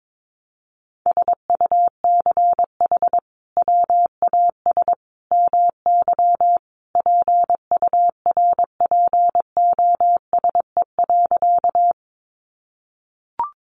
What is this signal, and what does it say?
Morse code, heard as a keyed tone, about 22 wpm, 700 Hz. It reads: SUCH WAS MY PURPOSE.
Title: Sentences from Top 800 Words 22wpm